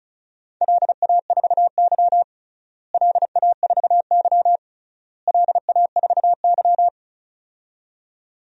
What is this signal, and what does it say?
Morse code, heard as a keyed tone, about 34 wpm, 700 Hz. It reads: LA4Y LA4Y LA4Y
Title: DXCC-100 Call Signs - 4 Characters - Repeated 3x - 35wpm